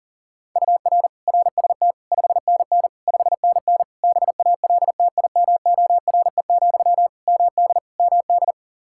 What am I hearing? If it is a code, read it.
Morse code, heard as a keyed tone, about 40 wpm, 700 Hz. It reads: UR RST 5NN 5NN BALTIMORE, MD MD
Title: Practice QSO - Exchange 1 - Beginning - 40wpm